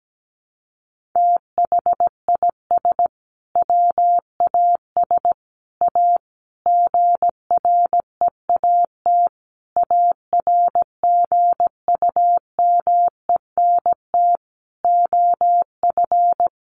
Morse code, heard as a keyed tone, about 17 wpm, 700 Hz. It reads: THIS WAS A GREAT ARGUMENT OF
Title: Sentences from Top 1400 Words 17wpm